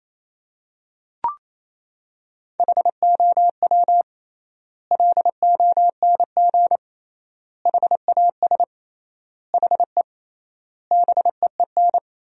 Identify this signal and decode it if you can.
Morse code, heard as a keyed tone, about 28 wpm, 700 Hz. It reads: HOW LONG HAS HE BEEN
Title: 3x Word-Spacing - Sentences from Top 100 Words - 28wpm